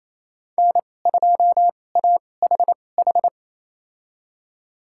Morse code, heard as a keyed tone, about 28 wpm, spaced as for 20 wpm, 700 Hz. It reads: N2AHH
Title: USA Call Signs - 2x Character Spacing - 28wpm